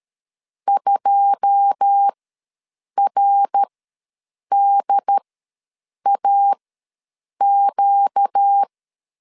Morse code, heard as a keyed tone, about 13 wpm, spaced as for 7 wpm, 800 Hz. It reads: 2RDAQ
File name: LEZIONI / 9 2 ws 12wpm 3406